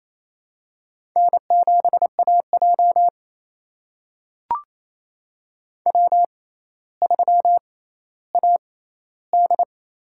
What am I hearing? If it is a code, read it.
Morse code, heard as a keyed tone, about 28 wpm, spaced as for 14 wpm, 700 Hz. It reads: N7AJ W3AD
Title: USA Call Signs - 6x Character Spacing - 28wpm